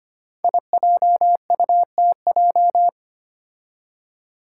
Morse code, heard as a keyed tone, about 25 wpm, 700 Hz. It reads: IJUTJ
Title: Five Challenging Letter-Number Combinations 25wpm